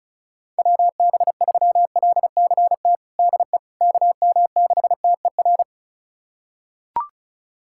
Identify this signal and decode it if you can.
Morse code, heard as a keyed tone, about 36 wpm, 700 Hz. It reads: WB3LCT DE KM6TER
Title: USA Callsign DE Callsign 35wpm